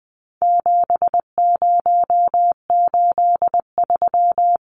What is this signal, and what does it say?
Morse code, heard as a keyed tone, about 20 wpm, 700 Hz. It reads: 7083
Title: Numbers - 4-Digit 20wpm